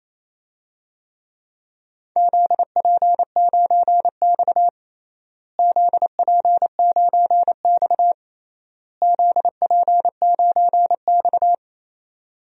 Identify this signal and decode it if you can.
Morse code, heard as a keyed tone, about 28 wpm, 700 Hz. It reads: ZP9X ZP9X ZP9X
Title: DXCC-100 Call Signs - 4 Characters - Repeated 3x - 28wpm